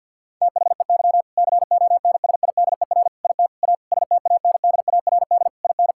Morse code, heard as a keyed tone, about 50 wpm, 700 Hz. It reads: THEY CONSIDER IT A STANDARD IN
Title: Sentences from Top 1000 Words - Review Part 6 - 50wpm